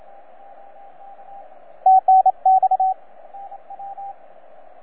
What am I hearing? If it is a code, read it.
Morse code, heard as a keyed tone, about 29 wpm, 750 Hz, with radio noise. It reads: TNX
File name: EF8M-WAEDC-CW-2010-20-CW